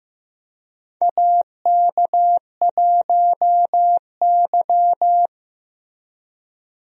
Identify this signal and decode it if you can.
Morse code, heard as a keyed tone, about 15 wpm, 700 Hz. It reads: AK1Y
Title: USA Call Signs - 4 Characters - Repeated 3x - 15wpm